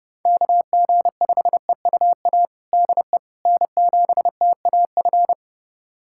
Morse code, heard as a keyed tone, about 30 wpm, 700 Hz. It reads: KG5EUA DE N7TAF
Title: USA Callsign DE Callsign 30wpm